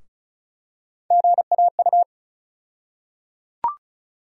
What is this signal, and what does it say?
Morse code, heard as a keyed tone, about 34 wpm, 700 Hz. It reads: GAU
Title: USA Call Sign Postfixes - 3 Letters 35wpm